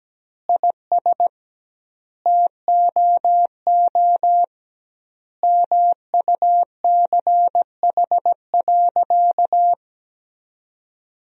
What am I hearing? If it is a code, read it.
Morse code, heard as a keyed tone, about 17 wpm, 700 Hz. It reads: IS TOO MUCH.
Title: 2x Word-Spacing - Sentences from Top 400 Words - 17wpm